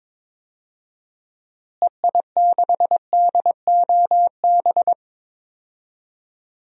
Morse code, heard as a keyed tone, about 22 wpm, 700 Hz. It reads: EI6DOB
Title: DXCC-100 Call Signs - 6 Characters - Repeated 3x - 22wpm